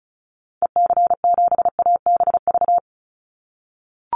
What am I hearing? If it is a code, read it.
Morse code, heard as a keyed tone, about 34 wpm, 700 Hz. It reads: EC7ABV
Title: Call Signs - 35wpm